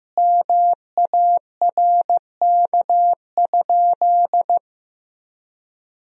Morse code, heard as a keyed tone, about 15 wpm, 700 Hz. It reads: MARK?
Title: Sentences from Top 300 Words and Abbreviations 15wpm